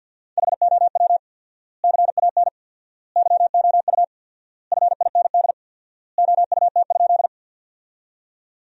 Morse code, heard as a keyed tone, about 52 wpm, 700 Hz. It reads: HOW CAN YOU FIND OUT?
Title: 4x Word-Spacing - Sentences from Top 200 Words - 50wpm